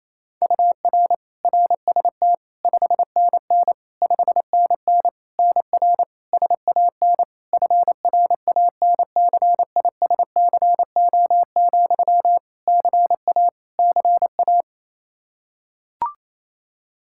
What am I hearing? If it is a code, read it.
Morse code, heard as a keyed tone, about 28 wpm, 700 Hz. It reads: UR RST 5NN 5NN NR SAN FRANCISCO, CA CA
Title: Practice QSO - Exchange 1 - Beginning - 28wpm